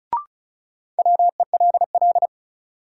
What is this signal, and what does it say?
Morse code, heard as a keyed tone, about 36 wpm, 700 Hz. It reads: WELL
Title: Sets of 2 Words - New Word Drill - 35wpm